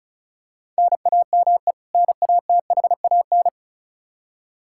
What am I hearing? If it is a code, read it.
Morse code, heard as a keyed tone, about 34 wpm, 700 Hz. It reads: NAME NATHAN
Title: Practice QSO - Exchange 1 - Beginning - 35wpm